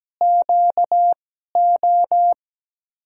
Morse code, heard as a keyed tone, about 17 wpm, 700 Hz. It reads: Q O
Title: USA Call Signs - 6 Characters - 2x Character Spacing - 17wpm